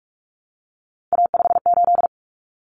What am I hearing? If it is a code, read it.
Morse code, heard as a keyed tone, about 44 wpm, 700 Hz. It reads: A58
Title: Common Cut Numbers - 3-Digit 45wpm